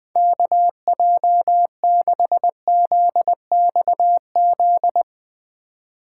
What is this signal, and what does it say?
Morse code, heard as a keyed tone, about 20 wpm, 700 Hz. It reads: KJ6ZXZ
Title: Speed Racing - USA 2x3 Call Signs 17wpm